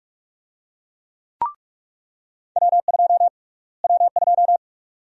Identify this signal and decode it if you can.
Morse code, heard as a keyed tone, about 46 wpm, 700 Hz. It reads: W2 W2
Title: USA Call Sign Prefixes - Repeated 2x - 45wpm